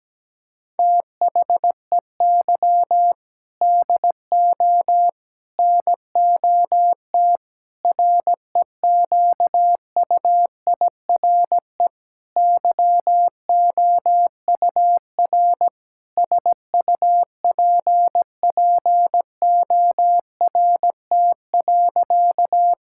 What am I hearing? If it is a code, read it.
Morse code, heard as a keyed tone, about 17 wpm, 700 Hz. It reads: THEY DO NOT REQUIRE YOUR SUPPORT.
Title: Sentences from Top 1000 Words - Review Part 6 - 17wpm